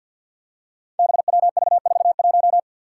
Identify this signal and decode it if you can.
Morse code, heard as a keyed tone, about 52 wpm, 700 Hz. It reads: BWV41
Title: Five Challenging Letter-Number Combinations 50wpm